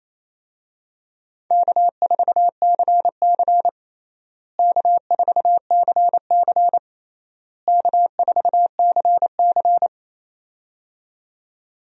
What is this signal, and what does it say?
Morse code, heard as a keyed tone, about 28 wpm, 700 Hz. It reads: K4CC K4CC K4CC
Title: USA Call Signs - 4 Characters - Repeated 3x - 28wpm